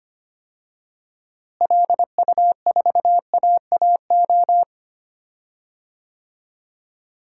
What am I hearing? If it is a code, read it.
Morse code, heard as a keyed tone, about 25 wpm, 700 Hz. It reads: LU4AAO
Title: DXCC-100 Call Signs - 6 Characters - Repeated 3x - 25wpm